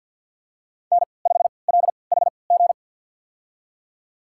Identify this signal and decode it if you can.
Morse code, heard as a keyed tone, about 53 wpm, spaced as for 29 wpm, 700 Hz. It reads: N5LHG
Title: USA Call Signs - 5 Characters - 3x Character Spacing - 50wpm